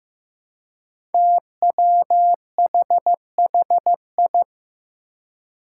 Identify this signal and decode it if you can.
Morse code, heard as a keyed tone, about 15 wpm, 700 Hz. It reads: TWHHI
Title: Five Challenging Letter-Number Combinations 15wpm